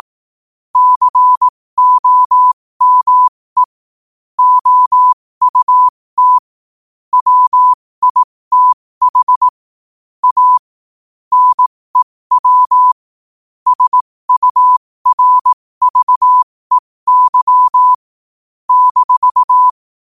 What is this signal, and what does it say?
Morse code, heard as a keyed tone, about 18 wpm, spaced as for 15 wpm, 1000 Hz. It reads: COME OUT WITH A NEW SURVEY -
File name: qotdfiles / 15wpm